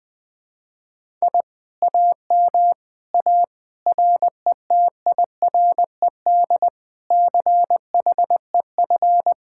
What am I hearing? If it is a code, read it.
Morse code, heard as a keyed tone, about 21 wpm, 700 Hz. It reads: I AM A RETIRED CHEF